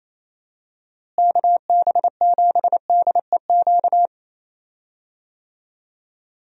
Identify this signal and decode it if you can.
Morse code, heard as a keyed tone, about 28 wpm, 700 Hz. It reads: KB7DEQ
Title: USA Call Signs - 6 Characters - Repeated 2x - 28wpm